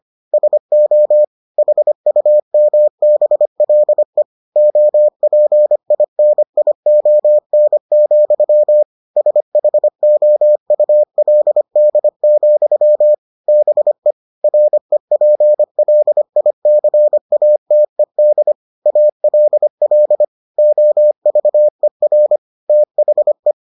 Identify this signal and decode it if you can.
Morse code, heard as a keyed tone, about 25 wpm, 600 Hz. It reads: SO HUMBLE OPINION, SHOULD, BE REPLICATED ALL OVER THE